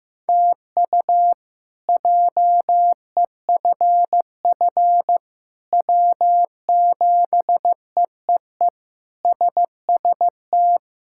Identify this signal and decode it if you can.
Morse code, heard as a keyed tone, about 16 wpm, 700 Hz. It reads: TU JEFF W7EEE SST
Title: K1USN SST QSO Practice - 15wpm